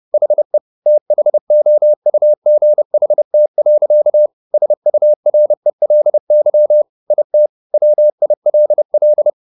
Morse code, heard as a keyed tone, about 30 wpm, 600 Hz. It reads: HE THOUGHT. SURELY IT WILL
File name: the old man and the sea / the old man and the sea 1 30wpm